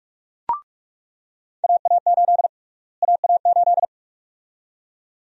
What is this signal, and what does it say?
Morse code, heard as a keyed tone, about 41 wpm, 700 Hz. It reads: AA8 AA8
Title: USA Call Sign Prefixes - Repeated 2x - 45wpm